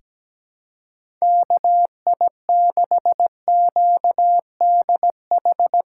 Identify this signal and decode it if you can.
Morse code, heard as a keyed tone, about 17 wpm, 700 Hz. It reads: KI6QDH
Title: K1USN SST QSO Practice - 17wpm